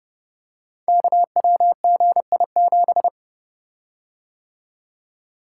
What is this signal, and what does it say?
Morse code, heard as a keyed tone, about 28 wpm, 700 Hz. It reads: KWGI7
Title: Code Groups - 5 Letter:Numbers - 30wpm